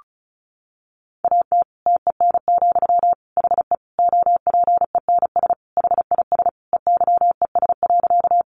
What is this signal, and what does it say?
Morse code, heard as a keyed tone, about 34 wpm, 700 Hz. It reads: AT TEN, HE OPENS HIS EYES.
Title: Easy Phrases 35wpm